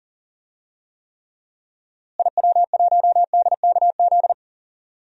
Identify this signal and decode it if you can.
Morse code, heard as a keyed tone, about 40 wpm, 700 Hz. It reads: IW1DKZ
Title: DXCC-100 Call Signs - 6 Characters - Repeated 3x - 40wpm